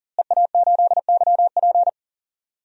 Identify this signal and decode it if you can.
Morse code, heard as a keyed tone, about 40 wpm, 700 Hz. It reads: EA8YP
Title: DXCC-100 Call Signs - 5 Characters - Repeated 3x - 40wpm